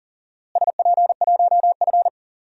Morse code, heard as a keyed tone, about 40 wpm, 700 Hz. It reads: SP1F
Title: DXCC-100 Call Signs - 4 Characters - Repeated 2x - 40wpm